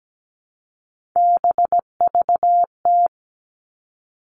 Morse code, heard as a keyed tone, about 17 wpm, 700 Hz. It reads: BVT
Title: All Cut Numbers - 3-Digit 17wpm